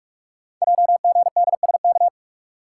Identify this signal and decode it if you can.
Morse code, heard as a keyed tone, about 44 wpm, 700 Hz. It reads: JGDSK
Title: Code Groups - 5 Letter:Numbers - 45wpm